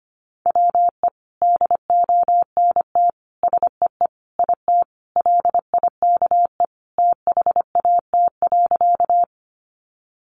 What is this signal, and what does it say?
Morse code, heard as a keyed tone, about 25 wpm, 700 Hz. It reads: WE DONT SEE IT LIKE THAT.
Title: Sentences from Top 300 Words 25wpm